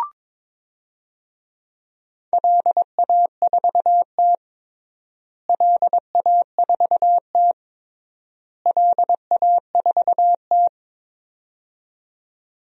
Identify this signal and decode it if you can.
Morse code, heard as a keyed tone, about 22 wpm, 700 Hz. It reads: LA4T LA4T LA4T
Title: DXCC-100 Call Signs - 4 Characters - Repeated 3x - 22wpm